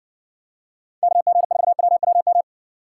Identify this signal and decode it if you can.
Morse code, heard as a keyed tone, about 60 wpm, 700 Hz. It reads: KG4WWM